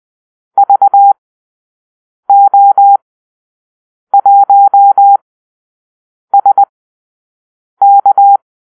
Morse code, heard as a keyed tone, about 20 wpm, spaced as for 6 wpm, 800 Hz. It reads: VO1SK
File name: arrl / Call-Sign Drill-6